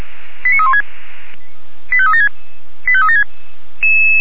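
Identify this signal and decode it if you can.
Five-tone selective call (ZVEI), heard as quick runs of five tones, tones of about 70 ms. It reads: 89316 85376 85376
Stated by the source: ZVEI five-tone selective call 89330, received by SDR.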